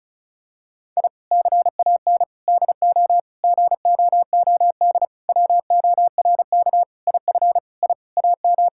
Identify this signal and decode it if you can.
Morse code, heard as a keyed tone, about 36 wpm, 700 Hz. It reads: I CAN DO GOOD WORK IF I AM
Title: Sentences from Top 1000 Words - Review Part 6 - 35wpm